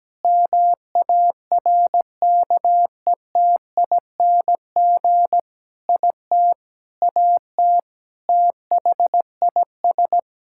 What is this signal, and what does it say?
Morse code, heard as a keyed tone, about 17 wpm, 700 Hz. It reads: MARKETING IT AT THIS
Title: Sentences from Top 1000 Words - Review Part 5 - 17wpm